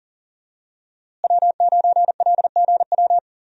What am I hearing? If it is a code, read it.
Morse code, heard as a keyed tone, about 40 wpm, 700 Hz. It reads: W9LGW
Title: USA Call Signs - 5 Characters - Repeated 2x - 40wpm